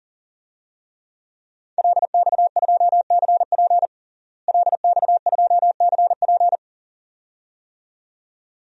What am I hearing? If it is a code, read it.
Morse code, heard as a keyed tone, about 40 wpm, 700 Hz. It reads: LX2CP LX2CP
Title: DXCC-100 Call Signs - 5 Characters - Repeated 2x - 40wpm